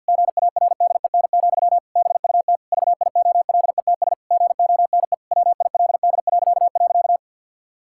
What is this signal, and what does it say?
Morse code, heard as a keyed tone, about 52 wpm, 700 Hz. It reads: GARDEN, BUT VIOLETS GONE WILD..
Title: Pangram Challenge v2 - 50wpm